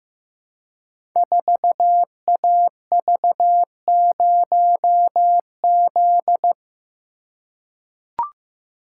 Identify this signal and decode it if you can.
Morse code, heard as a keyed tone, about 15 wpm, 700 Hz. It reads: 4AV0Z
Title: Code Groups - 5 Letter:Numbers - 15wpm